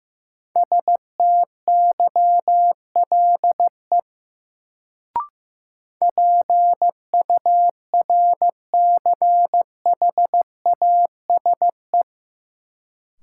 Morse code, heard as a keyed tone, about 15 wpm, 700 Hz. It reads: STYLE PURCHASE
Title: Sentences from Top 800 Words - New Word Drill - 15wpm